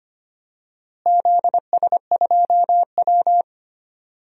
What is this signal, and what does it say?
Morse code, heard as a keyed tone, about 25 wpm, 700 Hz. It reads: ZS2W
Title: DXCC-100 Call Signs - 4 Characters - Repeated 3x - 25wpm